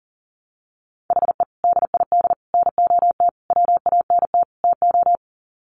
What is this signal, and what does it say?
Morse code, heard as a keyed tone, about 40 wpm, 700 Hz. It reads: HE DID NOT WANT TO
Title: Sets of 5 Words 40wpm